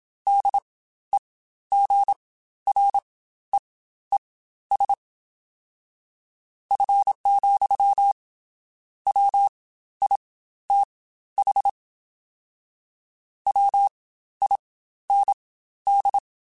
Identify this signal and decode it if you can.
Morse code, heard as a keyed tone, about 27 wpm, spaced as for 13 wpm, 800 Hz. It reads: DEGREES F, WITH WIND
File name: Spaced Characters / Spaced Characters-mp3s / Weather5 5 sp @25